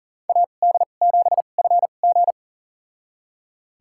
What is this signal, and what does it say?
Morse code, heard as a keyed tone, about 40 wpm, spaced as for 29 wpm, 700 Hz. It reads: AD7FG
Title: USA Call Signs - 5 Characters - 2x Character Spacing - 40wpm